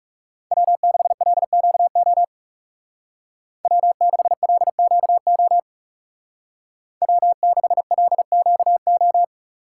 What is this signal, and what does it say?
Morse code, heard as a keyed tone, about 37 wpm, 700 Hz. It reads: W6LQO W6LQO W6LQO
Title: Speed Racing - USA Call Signs 30wpm